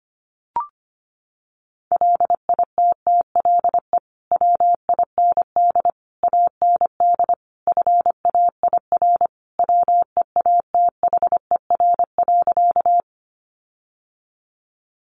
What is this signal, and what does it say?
Morse code, heard as a keyed tone, about 25 wpm, 700 Hz. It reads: LITTLE WIND AND FAIR WEATHER.
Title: Sentences from Top 1300 Words 25wpm